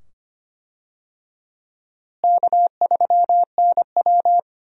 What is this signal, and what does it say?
Morse code, heard as a keyed tone, about 25 wpm, 700 Hz. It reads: K3NW
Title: USA Call Signs - 4 Characters - Repeated 2x - 25wpm